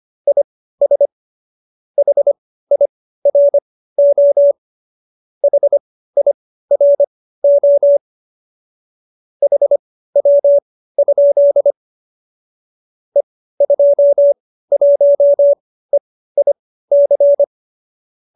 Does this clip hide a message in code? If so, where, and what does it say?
Yes: Morse code from 0.3 to 17.4 s, heard as a keyed tone, about 25 wpm, spaced as for 15 wpm, 570 Hz. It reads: IS HIRO HIRO HW? E21EIC